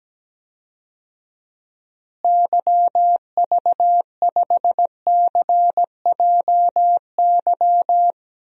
Morse code, heard as a keyed tone, about 17 wpm, 700 Hz. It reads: YV5CJY
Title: DXCC-100 Call Signs - 6 Characters - Repeated 2x - 17wpm